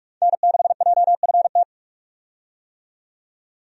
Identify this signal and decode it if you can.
Morse code, heard as a keyed tone, about 46 wpm, 700 Hz. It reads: N6JFT
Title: USA Call Signs - 5 Characters - 45wpm